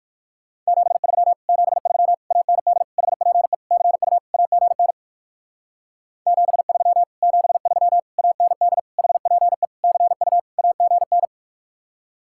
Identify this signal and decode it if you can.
Morse code, heard as a keyed tone, about 54 wpm, 690 Hz. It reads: 73 73 AND HPE CU AGN 73 73 AND HPE CU AGN
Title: Speed-Racing Common QSO Phrases - 45wpm